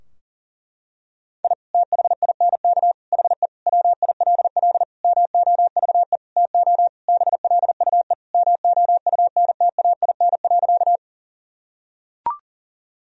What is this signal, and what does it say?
Morse code, heard as a keyed tone, about 40 wpm, 700 Hz. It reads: I THINK HE WILL MOVE TO BLUE MOUNTAIN.
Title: Sentences from Top 1000 Words - Review Part 3 - 40wpm